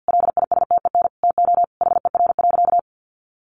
Morse code, heard as a keyed tone, about 50 wpm, 700 Hz. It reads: LISTEN TO HER.
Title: Sentences from Top 1300 Words 50wpm